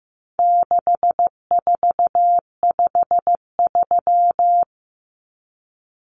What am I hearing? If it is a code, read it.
Morse code, heard as a keyed tone, about 15 wpm, 700 Hz. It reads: 6453
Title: Numbers - 4-Digit 15wpm